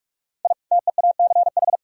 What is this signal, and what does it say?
Morse code, heard as a keyed tone, about 44 wpm, 700 Hz. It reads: I TEACH
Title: Sentences from Top 1000 Words - Review Part 6 - 45wpm